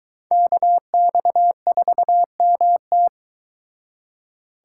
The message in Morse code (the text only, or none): KX4MT